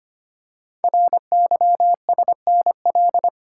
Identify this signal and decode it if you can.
Morse code, heard as a keyed tone, about 25 wpm, 700 Hz. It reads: RYSNL
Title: Code Groups - 5 Letters - 25wpm